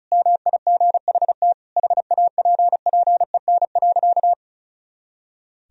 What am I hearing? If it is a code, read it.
Morse code, heard as a keyed tone, about 36 wpm, 700 Hz. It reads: MIGHT HAPPEN.
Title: Sentences from Top 400 Words and Abbreviations 35wpm